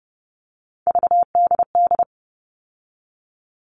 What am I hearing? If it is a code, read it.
Morse code, heard as a keyed tone, about 30 wpm, 700 Hz. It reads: VDD
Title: All Cut Numbers - 3-Digit 30wpm